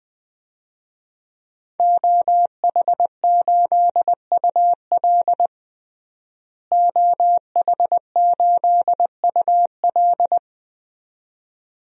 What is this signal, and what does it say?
Morse code, heard as a keyed tone, about 20 wpm, 700 Hz. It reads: OH8UL OH8UL
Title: DXCC-100 Call Signs - 5 Characters - Repeated 2x - 20wpm